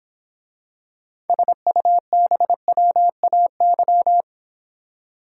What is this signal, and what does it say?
Morse code, heard as a keyed tone, about 26 wpm, 700 Hz. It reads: SUBWAY